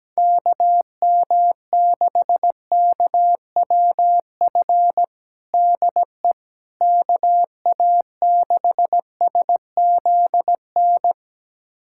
Morse code, heard as a keyed tone, about 17 wpm, 700 Hz. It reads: KM6KWF DE KA6SZN
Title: USA Callsign DE Callsign - 6 Characters - 17wpm